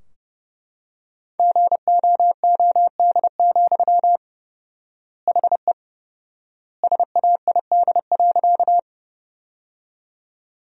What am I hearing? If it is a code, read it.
Morse code, heard as a keyed tone, about 30 wpm, 700 Hz. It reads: GOOD, HE SAID.
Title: 4x Word-Spacing - Sentences from Top 200 Words - 30wpm